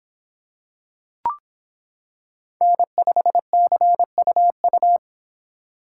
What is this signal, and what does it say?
Morse code, heard as a keyed tone, about 25 wpm, 700 Hz. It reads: N5CUU